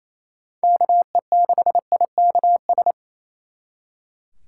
Morse code, heard as a keyed tone, about 35 wpm, 700 Hz. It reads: KE6IKS